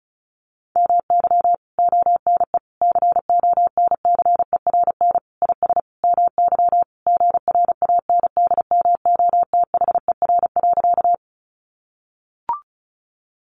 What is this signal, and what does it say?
Morse code, heard as a keyed tone, about 34 wpm, 700 Hz. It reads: MY ONE CONCERN IS MY GRANDMOTHER.
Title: Sentences from Top 1200 Words 35wpm